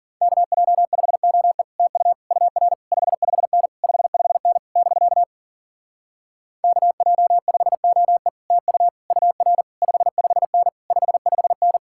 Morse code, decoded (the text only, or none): KJ5OE TU UR 55N 55N <BK> KJ5OE TU UR 55N 55N